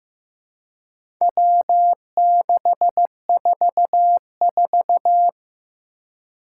Morse code, heard as a keyed tone, about 15 wpm, 700 Hz. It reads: W644
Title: Four Challenging Letter-Number Combinations 15wpm